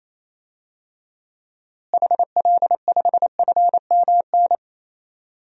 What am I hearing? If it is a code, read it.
Morse code, heard as a keyed tone, about 28 wpm, 700 Hz. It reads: HL5FMN
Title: DXCC-100 Call Signs - 6 Characters - Repeated 2x - 28wpm